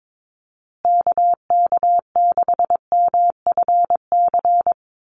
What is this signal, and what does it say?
Morse code, heard as a keyed tone, about 22 wpm, 700 Hz. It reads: KK6MFC
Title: USA Call Signs Repeated 2x 22wpm